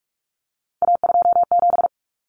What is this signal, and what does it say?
Morse code, heard as a keyed tone, about 44 wpm, 700 Hz. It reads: A27
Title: Common Cut Numbers - 3-Digit 45wpm